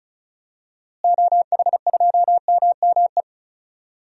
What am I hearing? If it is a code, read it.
Morse code, heard as a keyed tone, about 34 wpm, 700 Hz. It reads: OH2MME
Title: DXCC-100 Call Signs - 6 Characters - Repeated 3x - 35wpm